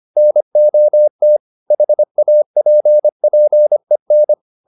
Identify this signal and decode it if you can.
Morse code, heard as a keyed tone, about 25 wpm, 600 Hz. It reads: NOT HAPPEN
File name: the old man and the sea part 1 25wpm